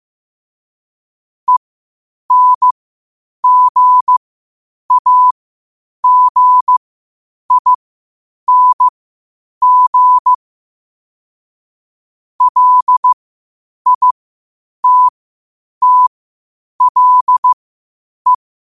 Morse code, heard as a keyed tone, about 15 wpm, spaced as for 8 wpm, 1000 Hz. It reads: ENGAGING LITTLE
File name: qotdfiles / 8wpm